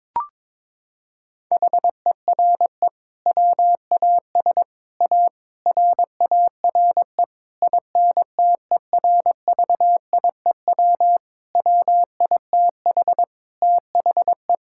Morse code, heard as a keyed tone, about 22 wpm, 700 Hz. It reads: HERE WAS A RARE INTERVIEW WITH THE
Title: Sentences from Top 1600 Words 22wpm